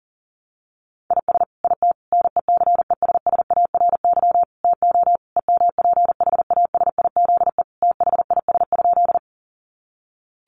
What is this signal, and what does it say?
Morse code, heard as a keyed tone, about 37 wpm, 700 Hz. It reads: IS IT NECESSARY TO EMPHASIZE THIS?
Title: Sentences from Top 1500 Words 40wpm